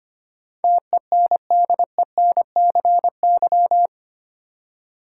Morse code, heard as a keyed tone, about 25 wpm, 690 Hz. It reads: TENDENCY
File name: 8 Letter Words - First 3 Letters Given - 25wpm